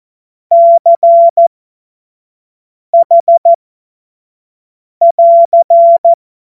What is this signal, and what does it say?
Morse code, heard as a keyed tone, about 14 wpm, spaced as for 5 wpm, 680 Hz. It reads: CH <AR>
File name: 25 F70